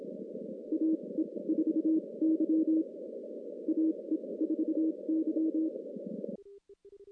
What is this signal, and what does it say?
Morse code, heard as a keyed tone, about 27 wpm, spaced as for 21 wpm, 320 Hz. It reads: AE4Y AE4Y
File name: contests / CW-SS-k5tr- -2100